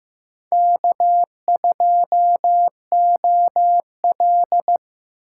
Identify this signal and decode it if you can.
Morse code, heard as a keyed tone, about 18 wpm, 700 Hz. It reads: K2OL
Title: Speed Racing - USA 1x2 Call Signs 15wpm